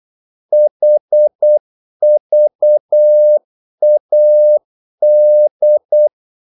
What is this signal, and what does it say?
Morse code, heard as a keyed tone, about 8 wpm, 600 Hz. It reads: HVAD
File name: Korte ord DK 8wpm